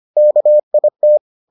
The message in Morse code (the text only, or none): KIT